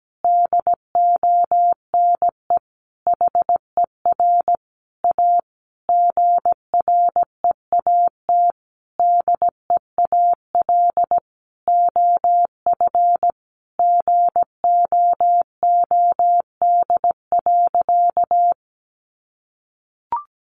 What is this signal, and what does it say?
Morse code, heard as a keyed tone, about 17 wpm, 700 Hz. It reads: DONE HER A GREAT DEAL OF GOOD.
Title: Sentences from Top 900 Words 17wpm